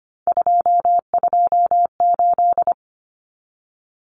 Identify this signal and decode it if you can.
Morse code, heard as a keyed tone, about 25 wpm, 700 Hz. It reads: 228
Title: Common Cut Numbers - 3-Digit 25wpm